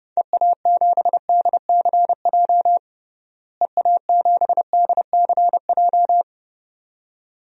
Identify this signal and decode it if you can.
Morse code, heard as a keyed tone, about 30 wpm, 700 Hz. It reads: EA7DCJ EA7DCJ
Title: DXCC-100 Call Signs - 6 Characters - Repeated 2x - 30wpm